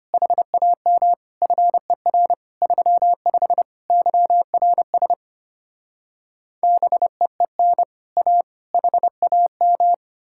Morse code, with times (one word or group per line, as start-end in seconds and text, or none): 0.1-1.1 HAM
1.4-2.3 FER
2.6-3.6 35
3.9-5.1 YRS
6.6-7.8 BEEN
8.2-8.4 A
8.7-9.9 HAM